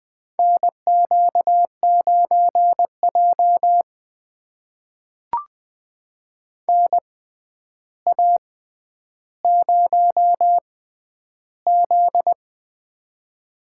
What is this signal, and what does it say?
Morse code, heard as a keyed tone, about 21 wpm, spaced as for 10 wpm, 700 Hz. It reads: NQ9J NA0Z
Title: USA Call Signs - 4 Characters - 6x Character Spacing - 20wpm